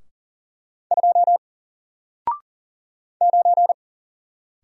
Morse code, heard as a keyed tone, about 40 wpm, 700 Hz. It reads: 2 9
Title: Single Character (Letter:Number:Punctuation:Prosign) Alt 40wpm